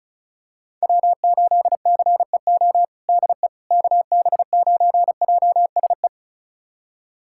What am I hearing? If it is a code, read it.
Morse code, heard as a keyed tone, about 34 wpm, 700 Hz. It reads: W8CEO DE KB9JSE
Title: USA Callsign DE Callsign 35wpm